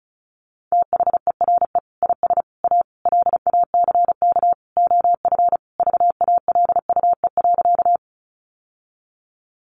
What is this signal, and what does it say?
Morse code, heard as a keyed tone, about 34 wpm, 700 Hz. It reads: THERE IS A LACK OF VALUE.
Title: Sentences from Top 1100 Words 35wpm